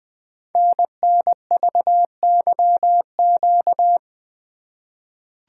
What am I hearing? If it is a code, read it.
Morse code, heard as a keyed tone, about 20 wpm, 700 Hz. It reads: NNVYQ
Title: Code Groups - 5 Letters - 20wpm